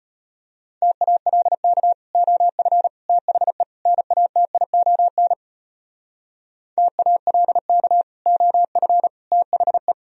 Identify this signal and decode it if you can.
Morse code, heard as a keyed tone, about 37 wpm, 700 Hz. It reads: TALK OF THE NATION TALK OF THE